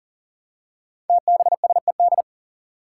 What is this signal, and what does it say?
Morse code, heard as a keyed tone, about 42 wpm, 700 Hz. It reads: TBSED
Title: Five Challenging Letter-Number Combinations 40wpm